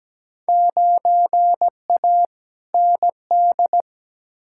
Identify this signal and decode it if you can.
Morse code, heard as a keyed tone, about 17 wpm, 700 Hz. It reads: 9A ND
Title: Field Day Exchanges Encore - 17wpm